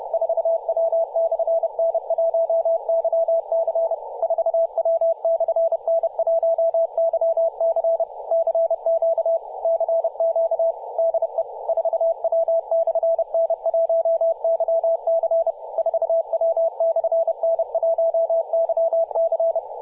Morse code, heard as a keyed tone, about 30 wpm, 660 Hz, with radio noise. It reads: 4W/N1YC 4W/N1YC CQ CQ DE 4W/N1YC 4W/N1YC
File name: posts / 2017-apr-04-0651z-4w-n1yc-18-077mhz-cw